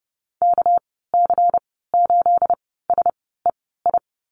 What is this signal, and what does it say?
Morse code, heard as a keyed tone, about 30 wpm, spaced as for 17 wpm, 700 Hz. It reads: KC8SEI
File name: Lesson 37g - Letter x - Call Signs 30wpm